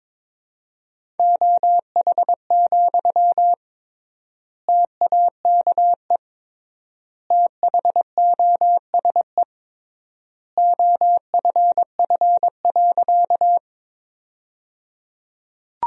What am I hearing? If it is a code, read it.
Morse code, heard as a keyed tone, about 22 wpm, 700 Hz. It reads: OH, TAKE THOSE OFF.
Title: 3x Word-Spacing - Sentences from Top 500 Words - 22wpm